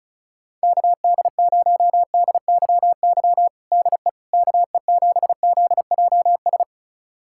Morse code, heard as a keyed tone, about 36 wpm, 700 Hz. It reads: KD0DYY DE KE7ZJS
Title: USA Callsign DE Callsign - 6 Characters - 35wpm